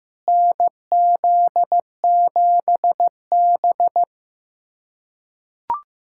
Morse code, heard as a keyed tone, about 15 wpm, 700 Hz. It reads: NZ7B
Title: USA Callsign DE Callsign - 4 Characters - 15wpm